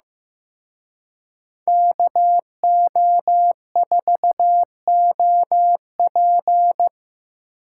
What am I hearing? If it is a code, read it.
Morse code, heard as a keyed tone, about 18 wpm, 700 Hz. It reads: KO4OP
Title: Speed Racing - USA 2x2 Call Signs 15wpm